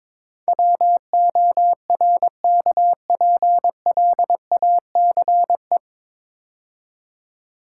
Morse code, heard as a keyed tone, about 22 wpm, 700 Hz. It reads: WORKPLACE
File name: 9 Letter Words - First 3 Letters Given - 22wpm